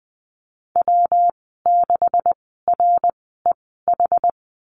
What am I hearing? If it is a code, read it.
Morse code, heard as a keyed tone, about 20 wpm, spaced as for 14 wpm, 700 Hz. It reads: W6REH
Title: 140 Lesson 23g - Letter f - Call Signs 20wpm